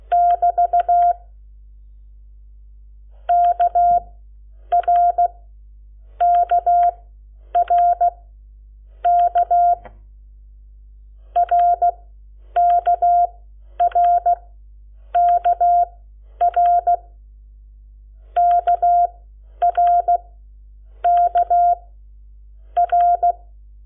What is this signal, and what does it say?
Morse code, heard as a keyed tone, about 16 wpm, spaced as for 9 wpm, 700 Hz. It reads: <BT> KRKRK RKRKR KRK R
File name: Lecciones / CLASE Nº7